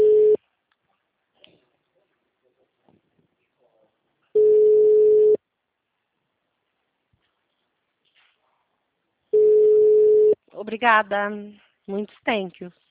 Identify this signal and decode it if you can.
Ringback tone (Europe, 425 Hz), with speech.